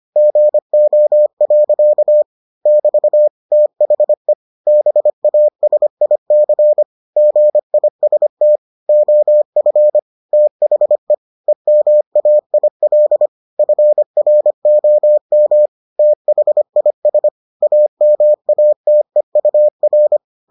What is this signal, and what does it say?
Morse code, heard as a keyed tone, about 25 wpm, 600 Hz. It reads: GO. <BT> THE BASIC GIST OF THE EMAIL FROM THIS AMATEUR